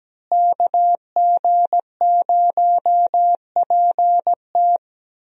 Morse code, heard as a keyed tone, about 17 wpm, 700 Hz. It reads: KG0PT
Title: USA Call Signs - 5 Characters - Repeated 3x - 17wpm